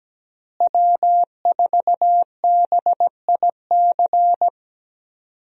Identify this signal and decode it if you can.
Morse code, heard as a keyed tone, about 17 wpm, 700 Hz. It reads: W4BIC
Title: USA Call Signs - 5 Characters - Repeated 3x - 17wpm